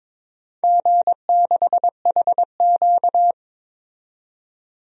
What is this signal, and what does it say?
Morse code, heard as a keyed tone, about 23 wpm, 700 Hz. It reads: G6HQ
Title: DXCC-100 Call Signs - 4 Characters - 22wpm